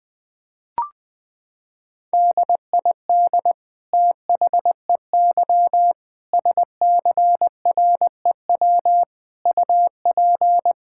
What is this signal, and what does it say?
Morse code, heard as a keyed tone, about 20 wpm, 700 Hz. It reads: DID THEY SCREW UP
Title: Sentences from Top 1600 Words 20wpm